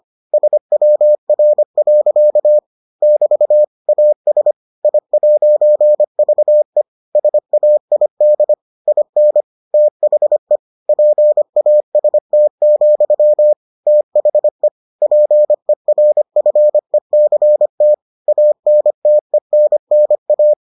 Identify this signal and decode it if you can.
Morse code, heard as a keyed tone, about 25 wpm, 600 Hz. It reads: SWR. <BT> AS I'VE SAID IN THE PAST, THE PERFECT ANTENNA